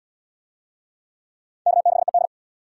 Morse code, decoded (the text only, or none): K6ED